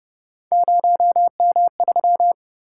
Morse code, heard as a keyed tone, about 30 wpm, 700 Hz. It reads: 0M3